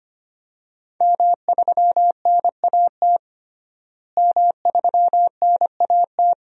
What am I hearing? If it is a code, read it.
Morse code, heard as a keyed tone, about 25 wpm, 700 Hz. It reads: M3NAT M3NAT
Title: DXCC-100 Call Signs - 5 Characters - Repeated 2x - 25wpm